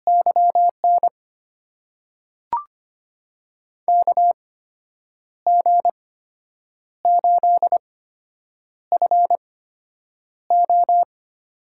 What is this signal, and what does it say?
Morse code, heard as a keyed tone, about 25 wpm, spaced as for 8 wpm, 700 Hz. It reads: YN KG8FO
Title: USA Call Signs - 5 Characters - 8x Character Spacing - 25wpm